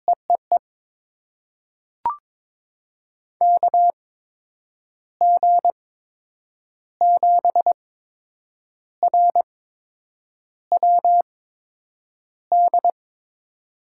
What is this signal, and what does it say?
Morse code, heard as a keyed tone, about 22 wpm, spaced as for 6 wpm, 700 Hz. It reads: E E E KG7RWD
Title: USA Call Signs - 6 Characters - 8x Character Spacing - 22wpm